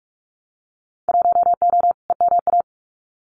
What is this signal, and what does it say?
Morse code, heard as a keyed tone, about 46 wpm, 700 Hz. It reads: 1O EMA
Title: Winter Field Day Exchanges Encore 45wpm